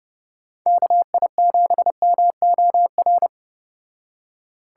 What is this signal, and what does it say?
Morse code, heard as a keyed tone, about 34 wpm, 700 Hz. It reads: KI7MOR